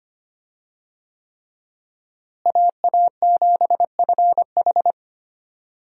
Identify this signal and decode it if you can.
Morse code, heard as a keyed tone, about 25 wpm, 700 Hz. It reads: AA7FH
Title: USA Call Signs - 5 Characters - Repeated 3x - 25wpm